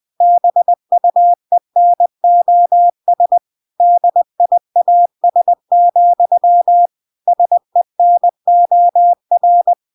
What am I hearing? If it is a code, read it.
Morse code, heard as a keyed tone, about 20 wpm, 700 Hz. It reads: BUENOS DIAS, SENOR
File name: recordings / 20wpm0000